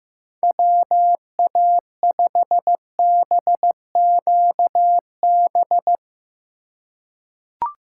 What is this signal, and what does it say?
Morse code, heard as a keyed tone, about 15 wpm, 700 Hz. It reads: WA5BQB
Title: USA Call Signs - 6 Characters - 2x Character Spacing - 15wpm